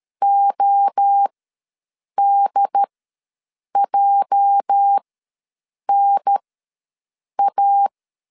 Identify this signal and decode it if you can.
Morse code, heard as a keyed tone, about 13 wpm, spaced as for 7 wpm, 800 Hz. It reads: ODJNA
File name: LEZIONI / 8 YJ45 / 8 YJ ws 12wpm 3342